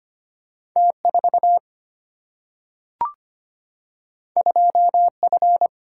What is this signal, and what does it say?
Morse code, heard as a keyed tone, about 25 wpm, 700 Hz. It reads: T 4 2 F
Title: Single-Letter Single-Number 25wpm